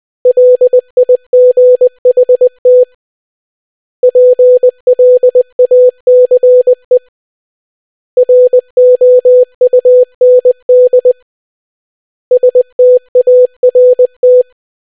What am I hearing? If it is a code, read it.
Morse code, heard as a keyed tone, about 20 wpm, 500 Hz. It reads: LIGHT PLACE ROUND START